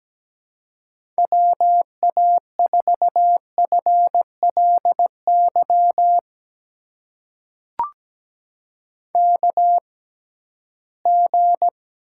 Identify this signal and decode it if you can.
Morse code, heard as a keyed tone, about 17 wpm, spaced as for 13 wpm, 700 Hz. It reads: WA4FLY KG
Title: USA Call Signs - 6 Characters - 6x Character Spacing - 17wpm